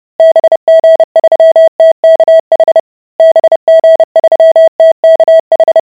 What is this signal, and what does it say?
Morse code, heard as a keyed tone, about 30 wpm, 650 Hz. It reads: BG3TKH BG3TKH